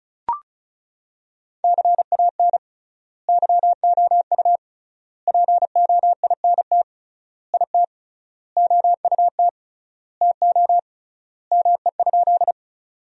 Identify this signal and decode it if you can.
Morse code, heard as a keyed tone, about 34 wpm, 700 Hz. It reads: CAN YOU POINT IT OUT TO ME?
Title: 3x Word-Spacing - Sentences from Top 200 Words - 35wpm